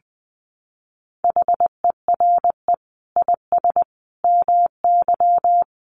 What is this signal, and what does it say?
Morse code, heard as a keyed tone, about 20 wpm, 700 Hz. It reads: HERE IS MY
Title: Sentences from Top 1200 Words 20wpm